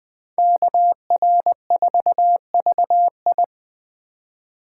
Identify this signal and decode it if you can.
Morse code, heard as a keyed tone, about 20 wpm, 700 Hz. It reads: KR4VI